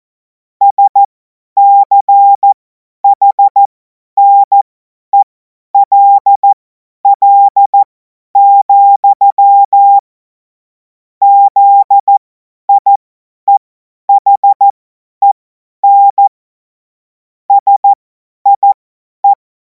Morse code, heard as a keyed tone, about 14 wpm, spaced as for 10 wpm, 800 Hz. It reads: SCHNELL, ZIEHEN SIE
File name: Morsekurs / Witze / Witz 14. April 2026 50 F70